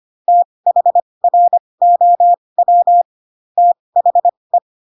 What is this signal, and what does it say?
Morse code, heard as a keyed tone, about 25 wpm, spaced as for 20 wpm, 700 Hz. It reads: THROW THE
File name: Pinocchio / Pinocchio ch7 15 25 ik1hge